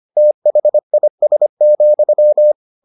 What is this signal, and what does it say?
Morse code, heard as a keyed tone, about 25 wpm, 600 Hz. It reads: THIS,